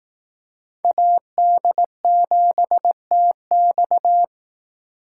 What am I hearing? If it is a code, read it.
Morse code, heard as a keyed tone, about 18 wpm, 700 Hz. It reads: AD7TX
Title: Speed Racing - USA 2x2 Call Signs 15wpm